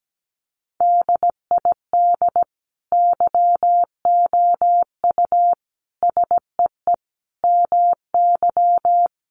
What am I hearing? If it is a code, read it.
Morse code, heard as a keyed tone, about 17 wpm, 700 Hz. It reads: DID YOU SEE MY
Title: Sentences from Top 1500 Words 17wpm